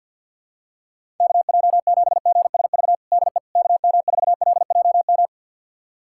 Morse code, heard as a keyed tone, about 51 wpm, 700 Hz. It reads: KJ7GSV DE KM4LJM
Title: USA Callsign DE Callsign - 6 Characters - 50wpm